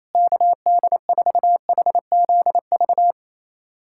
Morse code, heard as a keyed tone, about 28 wpm, 700 Hz. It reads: KD4HZV
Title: USA Callsign DE Callsign 28wpm